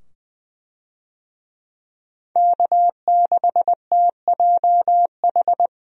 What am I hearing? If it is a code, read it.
Morse code, heard as a keyed tone, about 20 wpm, 700 Hz. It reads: K6TJH
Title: USA Call Signs - 5 Characters - 20wpm